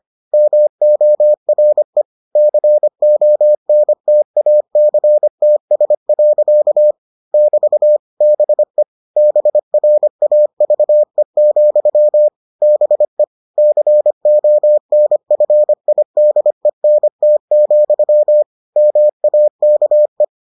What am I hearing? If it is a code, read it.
Morse code, heard as a keyed tone, about 25 wpm, 600 Hz. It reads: MORE CONTACTS. <BT> BE BRAVE, BE CONFIDENT, MAKE